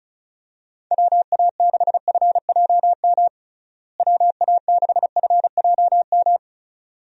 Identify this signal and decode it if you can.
Morse code, heard as a keyed tone, about 36 wpm, 700 Hz. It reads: WA6FJM WA6FJM
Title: USA Call Signs - 6 Characters - Repeated 2x - 35wpm